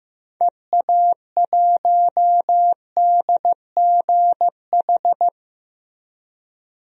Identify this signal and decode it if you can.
Morse code, heard as a keyed tone, about 15 wpm, 700 Hz. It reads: EA1DGH
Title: DXCC-100 Call Signs - 6 Characters - Repeated 3x - 15wpm